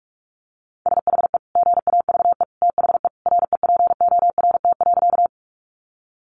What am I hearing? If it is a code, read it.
Morse code, heard as a keyed tone, about 44 wpm, 700 Hz. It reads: SHE GAVE THE REPORT.
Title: Sentences from Top 900 Words 45wpm